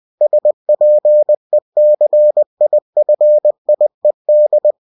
Morse code, heard as a keyed tone, about 20 wpm, 600 Hz. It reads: SPECIFIED
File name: cw-books / winnie-the-pooh / winnie-20wpm-page-0116